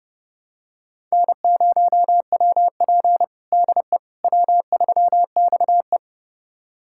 Morse code, heard as a keyed tone, about 30 wpm, 700 Hz. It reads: N0WP DE W3XE
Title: USA Callsign DE Callsign - 4 Characters - 30wpm